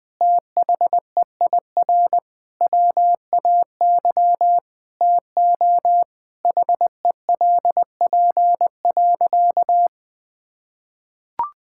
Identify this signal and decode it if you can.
Morse code, heard as a keyed tone, about 20 wpm, 700 Hz. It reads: THEIR WAY TO HELP.
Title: Sentences with 6-Word Phrases - 20wpm